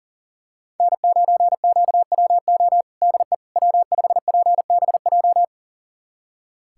Morse code, heard as a keyed tone, about 40 wpm, 700 Hz. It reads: N9QWO DE W5PBJ
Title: USA Callsign DE Callsign - 5 Characters - 40wpm